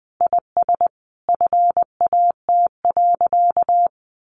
Morse code, heard as a keyed tone, about 20 wpm, 700 Hz. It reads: IS FAT.
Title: Easy Phrases 20wpm